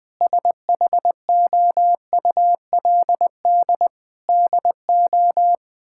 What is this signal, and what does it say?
Morse code, heard as a keyed tone, about 21 wpm, 700 Hz. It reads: SHOULD DO
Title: Sentences from Top 1600 Words 20wpm